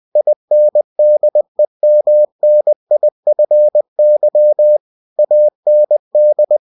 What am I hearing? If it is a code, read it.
Morse code, heard as a keyed tone, about 20 wpm, 600 Hz. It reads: INDEMNIFY AND
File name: cw-books / winnie-the-pooh / winnie-20wpm-page-0119